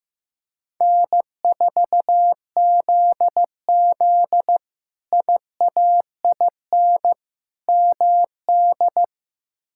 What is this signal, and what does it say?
Morse code, heard as a keyed tone, about 15 wpm, 700 Hz. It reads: N4ZZ IAIN MD
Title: K1USN SST QSO Practice - 15wpm